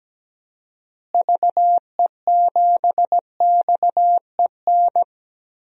Morse code, heard as a keyed tone, about 17 wpm, 700 Hz. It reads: VE7XEN
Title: DXCC-100 Call Signs - 6 Characters - Repeated 2x - 17wpm